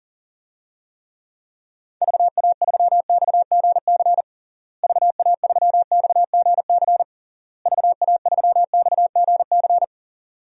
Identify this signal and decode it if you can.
Morse code, heard as a keyed tone, about 40 wpm, 700 Hz. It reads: VA3XGC VA3XGC VA3XGC
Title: DXCC-100 Call Signs - 6 Characters - Repeated 3x - 40wpm